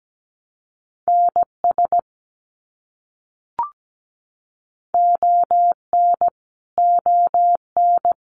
Morse code, heard as a keyed tone, about 17 wpm, 700 Hz. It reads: NS ON ON
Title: Canadian Provinces and Territories - Repeated 3x - 17wpm